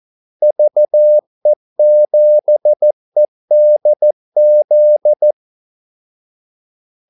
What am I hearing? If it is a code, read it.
Morse code, heard as a keyed tone, about 14 wpm, 600 Hz. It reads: VE7EDZ
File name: Callsigns-14wpm